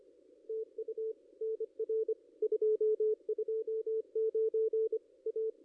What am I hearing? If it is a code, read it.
Morse code, heard as a keyed tone, about 25 wpm, 440 Hz, with radio noise. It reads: TU NR 229 A
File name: contests / ss-cw-k5tr-op-n5rz -1500